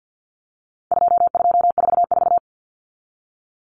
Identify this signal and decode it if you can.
Morse code, heard as a keyed tone, about 50 wpm, 700 Hz. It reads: 2244
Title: Common Cut Numbers - 4-Digit 50wpm